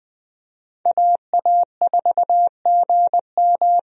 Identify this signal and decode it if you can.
Morse code, heard as a keyed tone, about 20 wpm, 700 Hz. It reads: AA4GM
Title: USA Call Signs - 5 Characters - 20wpm